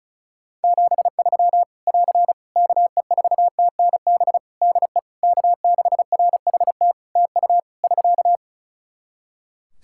Morse code, heard as a keyed tone, about 34 wpm, 700 Hz. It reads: 73 <AR> KE4TNB DE K6RHT TU <SK>
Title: Practice QSO - Exchange 4 - Ending - 35wpm